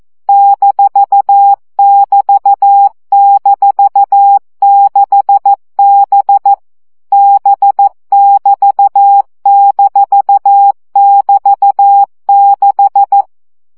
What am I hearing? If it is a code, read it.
Morse code, heard as a keyed tone, about 14 wpm, 800 Hz. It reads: -<BT>-6B B<BT>-<BT>6
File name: CW gyakorlas b 6 egyenlo minusz 12 baud